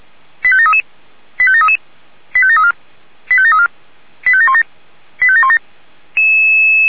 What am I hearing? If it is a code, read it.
Five-tone selective call (ZVEI), heard as quick runs of five tones, tones of about 70 ms. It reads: 85733 85733 85734 85734 85717 85717